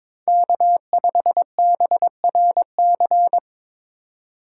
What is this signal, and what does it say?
Morse code, heard as a keyed tone, about 26 wpm, 700 Hz. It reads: K5BRC
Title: Speed Racing - USA 1x3 Call Signs 22wpm